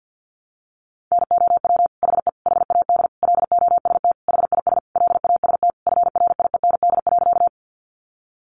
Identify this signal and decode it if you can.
Morse code, heard as a keyed tone, about 50 wpm, 700 Hz. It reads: NOW HE HAD LOST HIS LAST FRIEND.
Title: Sentences from Top 1400 Words 50wpm